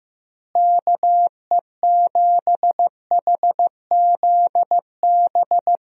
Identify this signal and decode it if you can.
Morse code, heard as a keyed tone, about 15 wpm, 700 Hz. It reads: KE7HZB
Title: USA Call Signs - 6 Characters - Repeated 2x - 15wpm